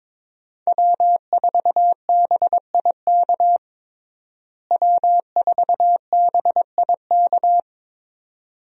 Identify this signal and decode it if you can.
Morse code, heard as a keyed tone, about 22 wpm, 700 Hz. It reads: W4BIK W4BIK
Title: USA Call Signs - 5 Characters - Repeated 2x - 22wpm